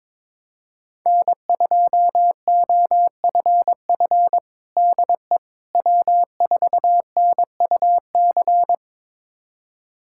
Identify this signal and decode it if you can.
Morse code, heard as a keyed tone, about 22 wpm, 700 Hz. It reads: N2OFF DE W4NUC
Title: USA Callsign DE Callsign - 5 Characters - 22wpm